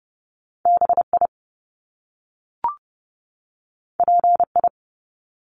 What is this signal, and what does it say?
Morse code, heard as a keyed tone, about 30 wpm, 700 Hz. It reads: BI PI
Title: Top Two Letter Combinations in Words 30wpm